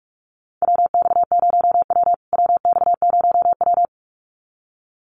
Morse code, heard as a keyed tone, about 44 wpm, 700 Hz. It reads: WX0W WX0W
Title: USA Call Signs Repeated 2x 45wpm